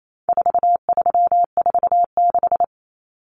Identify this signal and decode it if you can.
Morse code, heard as a keyed tone, about 28 wpm, 700 Hz. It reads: 4346
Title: Numbers - 4-Digit 28wpm